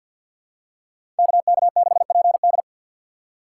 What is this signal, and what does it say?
Morse code, heard as a keyed tone, about 47 wpm, 700 Hz. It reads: KK6PD
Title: USA Call Signs - 2x Character Spacing - 50wpm